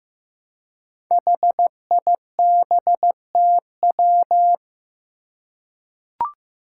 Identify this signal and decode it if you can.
Morse code, heard as a keyed tone, about 15 wpm, 700 Hz. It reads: HIBTW
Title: Five Challenging Letter-Number Combinations 15wpm